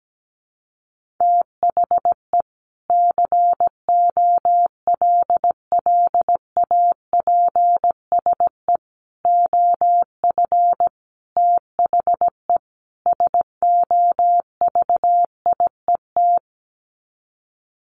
Morse code, heard as a keyed tone, about 17 wpm, 700 Hz. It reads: THE COLLAPSE OF THE SOVIET
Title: Sets of 5 Words Encore 17wpm